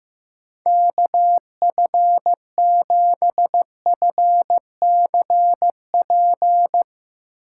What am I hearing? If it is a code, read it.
Morse code, heard as a keyed tone, about 16 wpm, 700 Hz. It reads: KF7FCP
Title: Speed Racing - USA Call Signs 15wpm